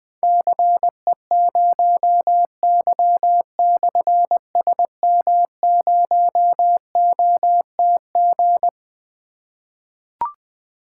Morse code, heard as a keyed tone, about 20 wpm, 700 Hz. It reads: CE0Y/SM0OTG
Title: Difficult Call Signs 20wpm